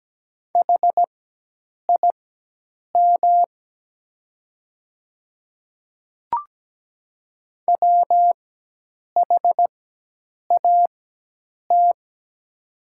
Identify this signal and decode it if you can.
Morse code, heard as a keyed tone, about 17 wpm, spaced as for 8 wpm, 700 Hz. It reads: HIM WHAT
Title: 4x Character Spacing - Top 100 Words - 17wpm